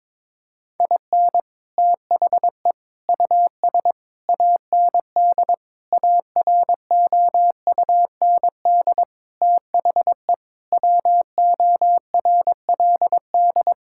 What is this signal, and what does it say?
Morse code, heard as a keyed tone, about 22 wpm, 700 Hz. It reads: IN THE US AND AROUND THE WORLD
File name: practice / 22wpm / Sets of 7 Words - 22wpm